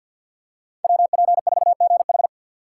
Morse code, heard as a keyed tone, about 52 wpm, 700 Hz. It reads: WP4GH
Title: USA Call Signs - 2x Character Spacing - 50wpm